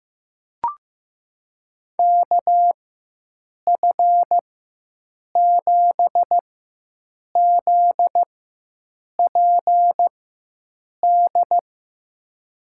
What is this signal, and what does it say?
Morse code, heard as a keyed tone, about 15 wpm, spaced as for 7 wpm, 700 Hz. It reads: KF7ZPD
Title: USA Call Signs - 4x Character Spacing - 15wpm